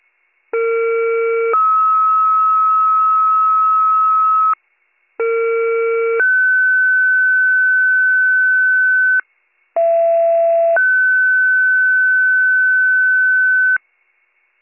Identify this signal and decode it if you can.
Two-tone page, heard as a one-second tone, then a three-second tone, about 460 then 1290 Hz, about 460 then 1500 Hz, about 670 then 1500 Hz. It is heard from 0.5 to 13.8 s.